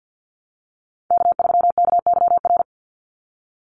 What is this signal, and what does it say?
Morse code, heard as a keyed tone, about 50 wpm, 700 Hz. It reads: K3KYR
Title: NA SOTA Chaser Callsigns 50wpm